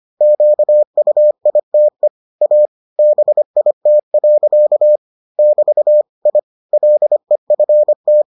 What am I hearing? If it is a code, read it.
Morse code, heard as a keyed tone, about 25 wpm, 600 Hz. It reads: QUITE A BIT. <BT> I LEFT